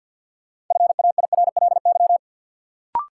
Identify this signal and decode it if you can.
Morse code, heard as a keyed tone, about 52 wpm, 700 Hz. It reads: FAIRLY
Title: Sentences from Top 1100 Words - New Word Drill - 50wpm